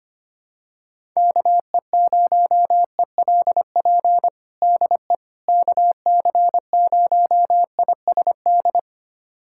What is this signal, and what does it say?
Morse code, heard as a keyed tone, about 25 wpm, 700 Hz. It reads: KE0ELP DE KC0ISD
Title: USA Callsign DE Callsign - 6 Characters - 25wpm